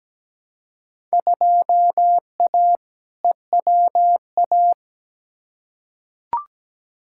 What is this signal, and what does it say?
Morse code, heard as a keyed tone, about 17 wpm, 700 Hz. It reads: 2A EWA
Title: Field Day Exchanges - 17wpm